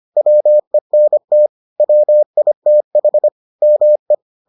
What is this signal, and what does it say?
Morse code, heard as a keyed tone, about 25 wpm, 600 Hz. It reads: WENT WITH ME